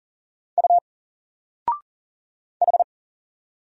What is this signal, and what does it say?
Morse code, heard as a keyed tone, about 40 wpm, 700 Hz. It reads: U H
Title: Single Letter-Number 40wpm